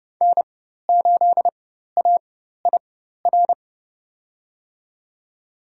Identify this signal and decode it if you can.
Morse code, heard as a keyed tone, about 31 wpm, spaced as for 14 wpm, 700 Hz. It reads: N8AIR
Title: USA Call Signs - 5 Characters - 4x Character Spacing - 30wpm